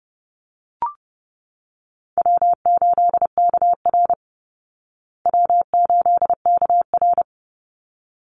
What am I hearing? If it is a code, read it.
Morse code, heard as a keyed tone, about 30 wpm, 700 Hz. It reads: W8KR W8KR
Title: Call Signs Repeated 3x - 30wpm